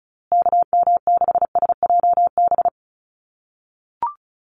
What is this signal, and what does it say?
Morse code, heard as a keyed tone, about 34 wpm, 700 Hz. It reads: KM6SJB
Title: USA Call Signs Repeated 3x 35wpm